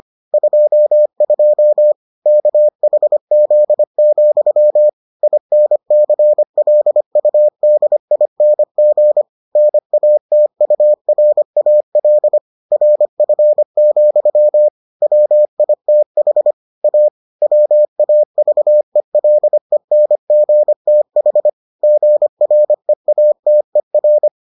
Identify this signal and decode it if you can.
Morse code, heard as a keyed tone, about 25 wpm, 600 Hz. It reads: 22 KHZ, INCLUDING NATURAL RF, WITH A WAVELENGTH GREATER